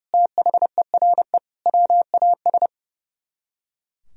Morse code, heard as a keyed tone, about 40 wpm, 700 Hz. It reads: THERE WAS